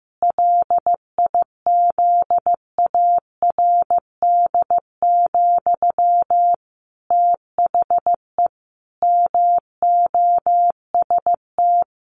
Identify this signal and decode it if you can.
Morse code, heard as a keyed tone, about 15 wpm, 700 Hz. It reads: LIZARD, THE MOST
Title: A Sound Of Thunder 15wpm